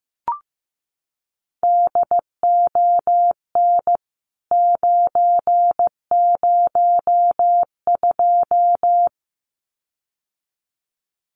Morse code, heard as a keyed tone, about 15 wpm, 700 Hz. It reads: DON 902
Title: CWT Exchange 15wpm